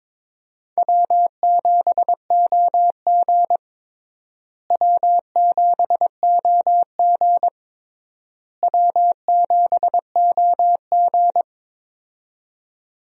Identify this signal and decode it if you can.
Morse code, heard as a keyed tone, about 22 wpm, 700 Hz. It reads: W7OG W7OG W7OG
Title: USA Call Signs - 4 Characters - Repeated 3x - 22wpm